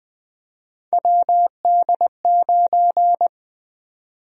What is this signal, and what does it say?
Morse code, heard as a keyed tone, about 20 wpm, 700 Hz. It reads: WD9
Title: USA Call Sign Prefixes - Repeated 2x - 20wpm